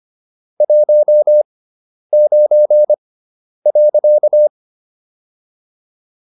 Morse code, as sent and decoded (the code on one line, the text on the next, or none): .---- ----. .-.-.-
19.